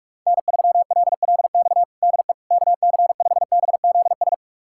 Morse code, heard as a keyed tone, about 46 wpm, 700 Hz. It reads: N3LLX DE KC5BZS
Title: USA Callsign DE Callsign 45wpm